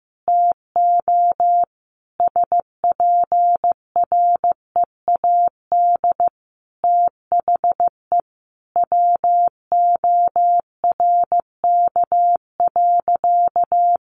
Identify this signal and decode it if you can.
Morse code, heard as a keyed tone, about 15 wpm, 700 Hz. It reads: TO SPREAD THE WORK.
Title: Sentences from Top 1400 Words 15wpm